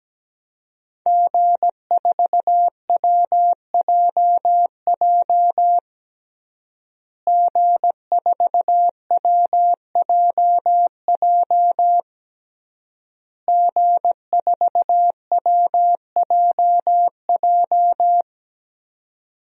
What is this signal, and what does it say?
Morse code, heard as a keyed tone, about 17 wpm, 700 Hz. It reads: G4WJJ G4WJJ G4WJJ
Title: DXCC-100 Call Signs - 5 Characters - Repeated 3x - 17wpm